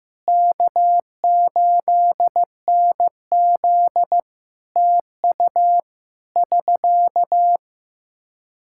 Morse code, heard as a keyed tone, about 16 wpm, 700 Hz. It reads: K8NZ TU <SK>
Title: Practice QSO - Exchange 4 - Ending - 15wpm